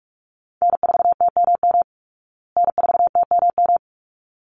Morse code, heard as a keyed tone, about 44 wpm, 700 Hz. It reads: N4TMM N4TMM
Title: Call Signs Repeated 3x - 45wpm